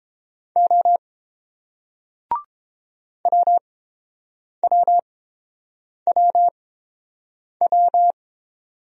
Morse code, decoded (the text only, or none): O WWWW